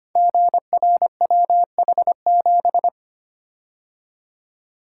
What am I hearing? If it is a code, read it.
Morse code, heard as a keyed tone, about 25 wpm, 700 Hz. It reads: GRWH7
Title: Code Groups - 5 Letter:Numbers - 25wpm